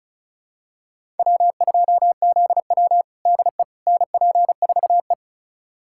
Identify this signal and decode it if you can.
Morse code, heard as a keyed tone, about 36 wpm, 700 Hz. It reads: W2ZW DE NP4E
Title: USA Callsign DE Callsign - 4 Characters - 35wpm